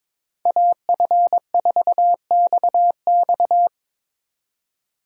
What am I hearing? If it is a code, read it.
Morse code, heard as a keyed tone, about 24 wpm, 700 Hz. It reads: AF4XX
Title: Speed Racing - USA 2x2 Call Signs 22wpm